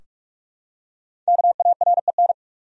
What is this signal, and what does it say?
Morse code, heard as a keyed tone, about 43 wpm, 700 Hz. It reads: KAREN
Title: USA Common Names - Max 5 Letters - 45wpm